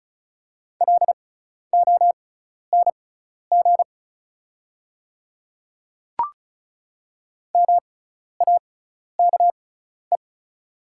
Morse code, heard as a keyed tone, about 36 wpm, spaced as for 12 wpm, 690 Hz. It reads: LONG MAKE
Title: 6x Character Spacing - Top 200 Words - 35wpm